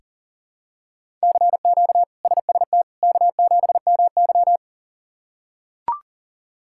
Morse code, heard as a keyed tone, about 40 wpm, 700 Hz. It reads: CQ SST K7MY
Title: K1USN SST QSO Practice - 40wpm